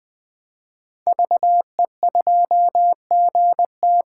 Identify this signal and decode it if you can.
Morse code, heard as a keyed tone, about 20 wpm, 700 Hz. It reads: VE2GT
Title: 2023 SOTA Chaser Call Signs - 20wpm